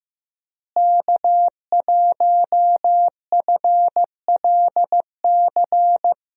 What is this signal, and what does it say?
Morse code, heard as a keyed tone, about 15 wpm, 700 Hz. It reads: K1FLC
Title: USA Call Signs - 5 Characters - Repeated 2x - 15wpm